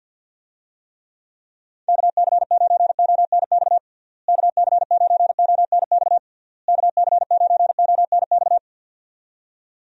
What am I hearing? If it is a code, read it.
Morse code, heard as a keyed tone, about 49 wpm, 700 Hz. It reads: KC9ONX KC9ONX KC9ONX
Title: USA Call Signs - 6 Characters - Repeated 3x - 50wpm